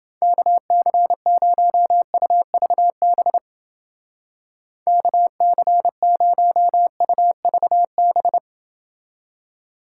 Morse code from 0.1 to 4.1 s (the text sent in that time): KC0UVB